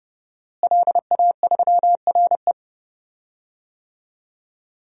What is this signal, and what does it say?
Morse code, heard as a keyed tone, about 32 wpm, 700 Hz. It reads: LA3RE
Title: DXCC-100 Call Signs - 5 Characters - Repeated 2x - 30wpm